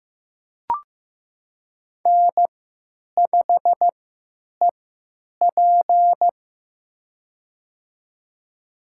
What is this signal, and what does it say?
Morse code, heard as a keyed tone, about 16 wpm, spaced as for 9 wpm, 700 Hz. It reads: N5EP
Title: USA Call Signs - 4 Characters - 3x Character Spacing - 15wpm